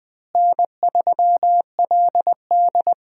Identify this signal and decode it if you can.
Morse code, heard as a keyed tone, about 20 wpm, 700 Hz. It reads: N3LD
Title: USA Call Signs - 4 Characters - Repeated 2x - 20wpm